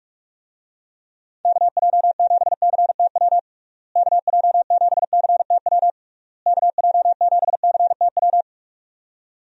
Morse code, heard as a keyed tone, about 46 wpm, 700 Hz. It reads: KJ7CTW KJ7CTW KJ7CTW
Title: USA Call Signs - 6 Characters - Repeated 3x - 45wpm